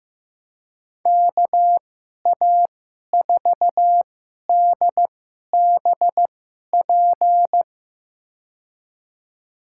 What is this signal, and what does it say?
Morse code, heard as a keyed tone, about 15 wpm, spaced as for 11 wpm, 700 Hz. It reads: KA4DBP
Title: USA Call Signs - 6 Characters - 2x Character Spacing - 15wpm